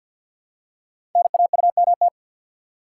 Morse code, heard as a keyed tone, about 48 wpm, 700 Hz. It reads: NAUMT
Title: Code Groups - 5 Letters - 50wpm